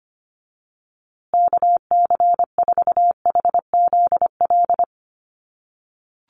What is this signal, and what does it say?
Morse code, heard as a keyed tone, about 25 wpm, 700 Hz. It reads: KC4HZL
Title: USA Call Signs Repeated 3x 25wpm